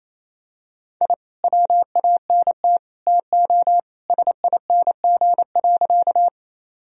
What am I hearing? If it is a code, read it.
Morse code, heard as a keyed tone, about 28 wpm, 700 Hz. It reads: I WANT TO SING.
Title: Sentences from Top 1000 Words - Review Part 2 - 28wpm